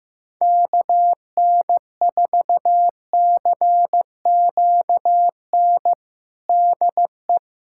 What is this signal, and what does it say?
Morse code, heard as a keyed tone, about 15 wpm, 700 Hz. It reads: KN4CQN DE
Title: USA Callsign DE Callsign - 6 Characters - 15wpm